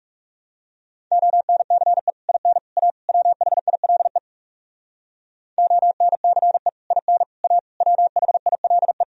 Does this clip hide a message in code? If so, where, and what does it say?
Yes: Morse code from 1.1 to 9.0 s, heard as a keyed tone, about 40 wpm, 700 Hz. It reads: ONCE IN A WHILE ONCE IN A WHILE